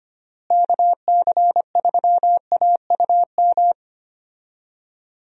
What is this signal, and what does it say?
Morse code, heard as a keyed tone, about 25 wpm, 700 Hz. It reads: KC3AUM
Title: USA Call Signs - 6 Characters - Repeated 2x - 25wpm